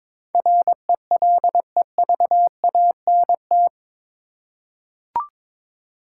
Morse code, heard as a keyed tone, about 22 wpm, 700 Hz. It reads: RELEVANT
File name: practice / 22wpm / Eight Letter Words - 22wpm